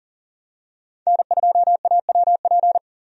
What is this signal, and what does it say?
Morse code, heard as a keyed tone, about 40 wpm, 700 Hz. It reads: N2AWP
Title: USA Call Signs - 5 Characters - 40wpm